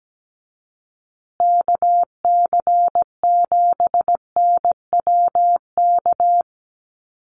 Morse code, heard as a keyed tone, about 17 wpm, 700 Hz. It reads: KC7NWK
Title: USA Call Signs Repeated 3x 17wpm